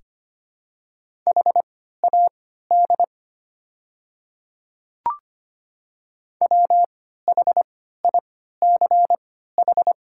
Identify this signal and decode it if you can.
Morse code, heard as a keyed tone, about 25 wpm, spaced as for 14 wpm, 700 Hz. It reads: HAD WHICH
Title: 3x Character Spacing - Top 100 Words - 25wpm